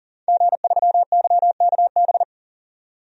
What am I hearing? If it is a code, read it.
Morse code, heard as a keyed tone, about 40 wpm, 700 Hz. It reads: G3YKB
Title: DXCC-100 Call Signs - 5 Characters - 40wpm